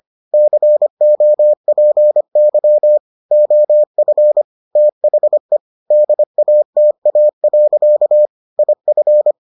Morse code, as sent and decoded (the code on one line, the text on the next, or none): -.-. --- .--. -.-- / --- ..-. / - .... / -.. .- - .- .-.-.- / .. ..-.
COPY OF THE DATA. IF